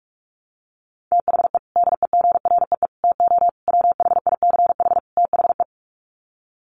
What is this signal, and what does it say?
Morse code, heard as a keyed tone, about 44 wpm, 700 Hz. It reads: THE DEGREE TO WHICH THE
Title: Sets of 5 Words Encore2 45wpm